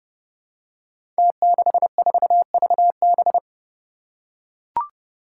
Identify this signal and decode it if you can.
Morse code, heard as a keyed tone, about 31 wpm, 700 Hz. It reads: T64VB
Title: Five Challenging Letter-Number Combinations 30wpm